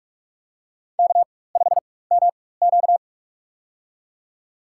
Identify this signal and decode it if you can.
Morse code, heard as a keyed tone, about 46 wpm, spaced as for 21 wpm, 700 Hz. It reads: K5MQ
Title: USA Call Signs - 4 Characters - 4x Character Spacing - 45wpm